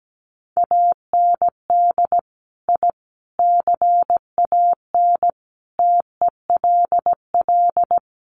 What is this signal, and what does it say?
Morse code, heard as a keyed tone, about 17 wpm, 700 Hz. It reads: AND I CAN TELL
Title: Sets of 5 Words Encore2 17wpm